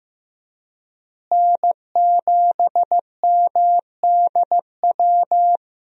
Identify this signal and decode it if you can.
Morse code, heard as a keyed tone, about 16 wpm, 700 Hz. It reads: N7MDW
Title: USA Call Signs - 5 Characters - Repeated 2x - 15wpm